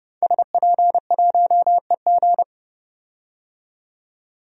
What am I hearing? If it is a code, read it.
Morse code, heard as a keyed tone, about 30 wpm, 700 Hz. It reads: SP1EG
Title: DXCC-100 Call Signs - 5 Characters - 30wpm